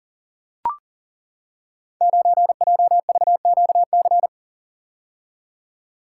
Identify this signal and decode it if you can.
Morse code, heard as a keyed tone, about 37 wpm, 700 Hz. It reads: 9JVQC
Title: Code Groups - 5 Letter:Numbers - 40wpm